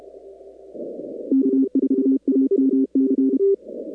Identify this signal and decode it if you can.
Morse code, heard as a keyed tone, about 33 wpm, 270 Hz. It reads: K4AMC
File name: contests / 2019-cw-wpx-kt5j-k5pi- -2330 0-41